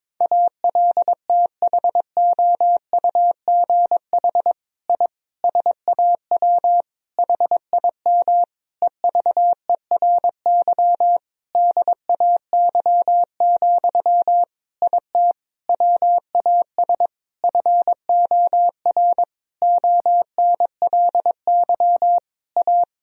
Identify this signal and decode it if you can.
Morse code, heard as a keyed tone, about 22 wpm, 700 Hz. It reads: ALTHOUGH I SAW HIM EVERY DAY, IT WAS FOR ONLY A
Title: Sentences from Top 1000 Words - Review Part 3 - 22wpm